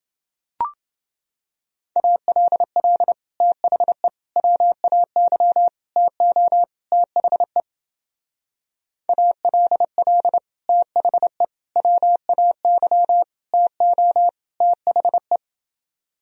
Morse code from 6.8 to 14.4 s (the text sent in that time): THE ALL THE WAY TO